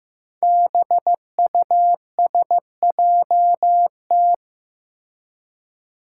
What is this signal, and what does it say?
Morse code, heard as a keyed tone, about 15 wpm, 700 Hz. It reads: BUSJT
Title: Five Challenging Letter-Number Combinations 15wpm